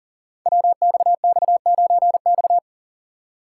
Morse code, heard as a keyed tone, about 40 wpm, 700 Hz. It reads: WXX9X
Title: Code Groups - 5 Letter:Numbers - 40wpm